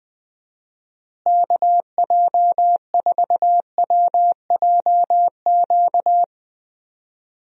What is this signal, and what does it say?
Morse code, heard as a keyed tone, about 20 wpm, 700 Hz. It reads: KJ4WJQ
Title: USA Call Signs - 6 Characters - Repeated 2x - 20wpm